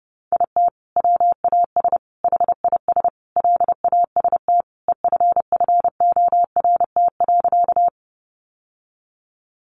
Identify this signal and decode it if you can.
Morse code, heard as a keyed tone, about 30 wpm, 700 Hz. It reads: IT WAS HIS LAST EFFORT.
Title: Sentences from Top 1200 Words 30wpm